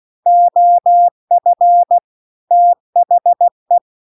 Morse code, heard as a keyed tone, about 16 wpm, 700 Hz. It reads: OF THE
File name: CW-The-Big-Twoheaded-River-1-16WPM